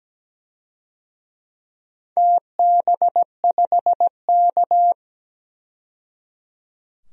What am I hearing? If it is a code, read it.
Morse code, heard as a keyed tone, about 17 wpm, 700 Hz. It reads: TB5K
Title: DXCC-100 Call Signs - 4 Characters - 17wpm